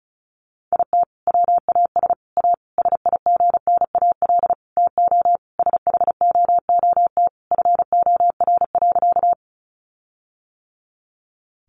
Morse code, heard as a keyed tone, about 34 wpm, 700 Hz. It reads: IT WAS A SIGNAL TO SHOOT FOR.
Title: Sentences from Top 1400 Words 35wpm